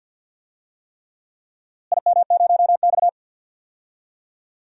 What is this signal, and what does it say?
Morse code, heard as a keyed tone, about 51 wpm, 700 Hz. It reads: IM0X
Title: DXCC-100 Call Signs - 4 Characters - Repeated 3x - 50wpm